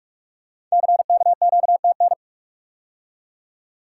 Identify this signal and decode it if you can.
Morse code, heard as a keyed tone, about 44 wpm, 700 Hz. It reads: CKQTN
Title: Code Groups - 5 Letters - 45wpm